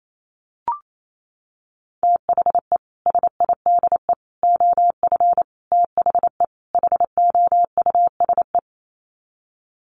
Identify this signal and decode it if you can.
Morse code, heard as a keyed tone, about 28 wpm, 700 Hz. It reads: THE SIDE OF THE HOUSE
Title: Sets of 5 Words Encore2 28wpm